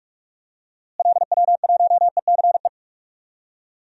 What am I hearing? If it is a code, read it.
Morse code, heard as a keyed tone, about 42 wpm, 700 Hz. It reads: LW1ECE
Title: DXCC-100 Call Signs - 6 Characters - Repeated 3x - 45wpm